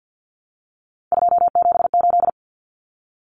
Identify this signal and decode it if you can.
Morse code, heard as a keyed tone, about 50 wpm, 700 Hz. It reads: 278
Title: Numbers - 3-Digit 50wpm